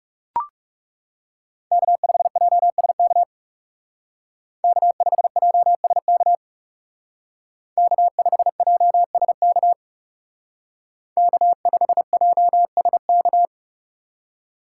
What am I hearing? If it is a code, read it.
Morse code, heard as a keyed tone, about 37 wpm, 700 Hz. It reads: K5JSK K5JSK K5JSK K5JSK
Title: Speed Racing - USA 1x3 Call Signs 30wpm